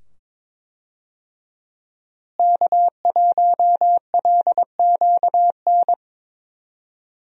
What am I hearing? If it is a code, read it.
Morse code, heard as a keyed tone, about 22 wpm, 700 Hz. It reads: K1LQN
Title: USA Call Signs - 5 Characters - Repeated 2x - 22wpm